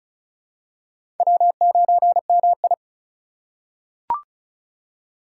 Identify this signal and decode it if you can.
Morse code, heard as a keyed tone, about 35 wpm, 700 Hz. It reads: W9MI
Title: USA Call Signs - 4 Characters - 2x Character Spacing - 35wpm